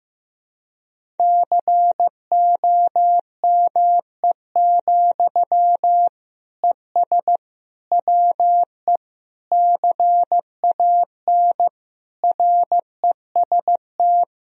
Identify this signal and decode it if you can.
Morse code, heard as a keyed tone, about 15 wpm, 700 Hz. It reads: COME, ES WE CAN REST
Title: Sentences from Top 500 Words and Abbreviations 15wpm